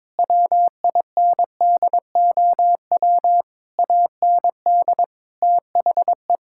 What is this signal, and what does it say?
Morse code, heard as a keyed tone, about 22 wpm, 700 Hz. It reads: WINDOW AND THE
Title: Sentences from Top 1600 Words 22wpm